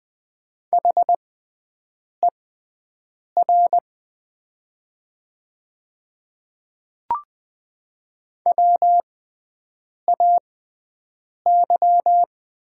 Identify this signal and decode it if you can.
Morse code, heard as a keyed tone, about 20 wpm, spaced as for 7 wpm, 700 Hz. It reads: HER WAY
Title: 6x Character Spacing - Top 300 Words - 20wpm